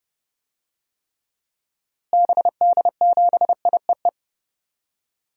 Morse code, heard as a keyed tone, about 30 wpm, 700 Hz. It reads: BD7IEE
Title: DXCC-100 Call Signs - 6 Characters - Repeated 2x - 30wpm